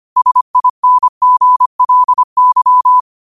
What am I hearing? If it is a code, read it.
Morse code, heard as a keyed tone, about 25 wpm, 1000 Hz. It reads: SINGLY
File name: qotdfiles / 25wpm